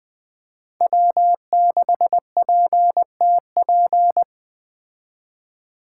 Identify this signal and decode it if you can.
Morse code, heard as a keyed tone, about 20 wpm, 700 Hz. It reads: W6PTP
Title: Speed Racing - USA 1x3 Call Signs 17wpm